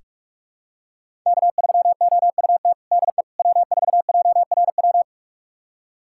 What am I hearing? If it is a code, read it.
Morse code, heard as a keyed tone, about 47 wpm, 700 Hz. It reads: K3OUT DE W4JRW
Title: USA Callsign DE Callsign - 5 Characters - 45wpm